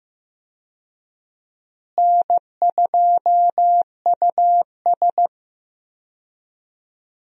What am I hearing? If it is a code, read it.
Morse code, heard as a keyed tone, about 15 wpm, 700 Hz. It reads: N2US
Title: USA Call Signs - 4 Characters - Repeated 3x - 15wpm